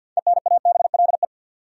Morse code, heard as a keyed tone, about 47 wpm, 700 Hz. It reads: ENABLE
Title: Sentences from Top 1500 Words - New Word Drill - 50wpm